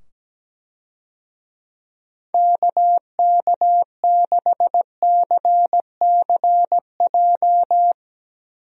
Morse code, heard as a keyed tone, about 17 wpm, 700 Hz. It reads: KK6CCJ
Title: USA Call Signs - 6 Characters - 17wpm